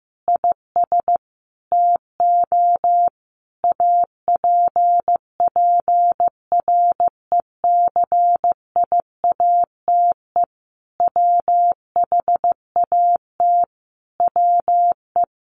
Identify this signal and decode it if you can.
Morse code, heard as a keyed tone, about 15 wpm, 700 Hz. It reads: IS TO APPRECIATE WHAT WE
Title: Sentences from Top 1300 Words 15wpm